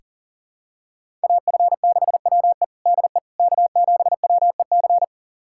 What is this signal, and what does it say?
Morse code, heard as a keyed tone, about 40 wpm, 700 Hz. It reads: AF6WE DE K7WEC
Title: USA Callsign DE Callsign - 5 Characters - 40wpm